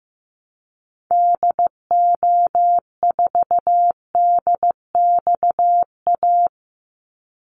Morse code, heard as a keyed tone, about 15 wpm, 700 Hz. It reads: DO4DXA
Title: Call Signs Repeated 3x - 15wpm